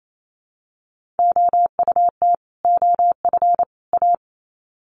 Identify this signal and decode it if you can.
Morse code, heard as a keyed tone, about 28 wpm, 700 Hz. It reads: OUT OF A
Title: Sets of 3 Words 28wpm